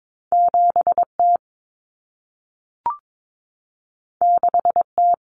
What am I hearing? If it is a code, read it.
Morse code, heard as a keyed tone, about 22 wpm, 700 Hz. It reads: 7T 6T
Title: Common Cut Numbers - 2-Digit 22wpm